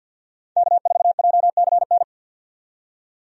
Morse code, heard as a keyed tone, about 47 wpm, 700 Hz. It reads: K4JCN
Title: USA Call Signs - 5 Characters - 50wpm